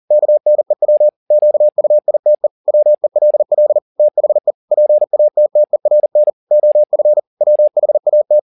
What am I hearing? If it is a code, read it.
Morse code, heard as a keyed tone, about 40 wpm, 600 Hz. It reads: KNEW QUITE WELL THE PATTERN OF WHAT
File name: the old man and the sea 2 40wpm